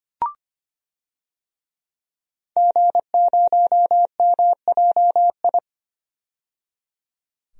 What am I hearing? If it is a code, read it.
Morse code, heard as a keyed tone, about 25 wpm, 700 Hz. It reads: G0MJI
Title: DXCC-100 Call Signs - 5 Characters - 25wpm